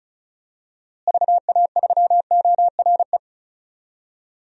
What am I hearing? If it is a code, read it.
Morse code, heard as a keyed tone, about 36 wpm, 700 Hz. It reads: VA3ORE
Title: DXCC-100 Call Signs - 6 Characters - Repeated 2x - 35wpm